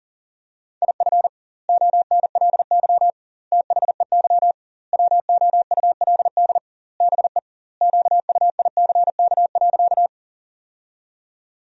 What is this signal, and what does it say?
Morse code, heard as a keyed tone, about 41 wpm, 700 Hz. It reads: IF ONLY THEY WOULD BE QUICK.
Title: 2x Word-Spacing - Sentences from Top 500 Words - 40wpm